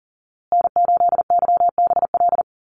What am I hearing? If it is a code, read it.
Morse code, heard as a keyed tone, about 40 wpm, 700 Hz. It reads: N8YBL
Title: USA Call Signs Repeated 3x 40wpm